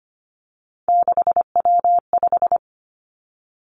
Morse code, heard as a keyed tone, about 25 wpm, 700 Hz. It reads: 6W5
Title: Three Challenging Letter-Number Combinations 25wpm